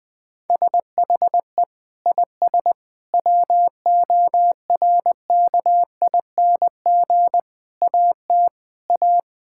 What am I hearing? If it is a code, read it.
Morse code, heard as a keyed tone, about 20 wpm, 700 Hz. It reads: SHE IS WORKING AT A
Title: Sentences from Top 1000 Words - Review Part 5 - 20wpm